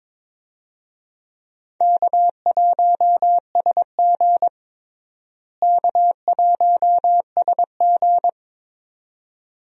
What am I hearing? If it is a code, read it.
Morse code, heard as a keyed tone, about 22 wpm, 700 Hz. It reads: K1SG K1SG
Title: USA Call Signs - 4 Characters - Repeated 2x - 22wpm